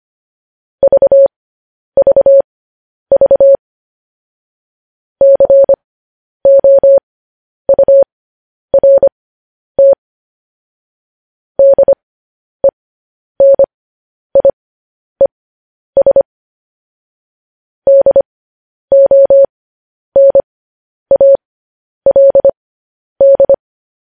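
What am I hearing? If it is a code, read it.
Morse code, heard as a keyed tone, about 25 wpm, spaced as for 10 wpm, 550 Hz. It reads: VVV COURT DENIES DONALD
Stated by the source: Morse code transcription of news headlines.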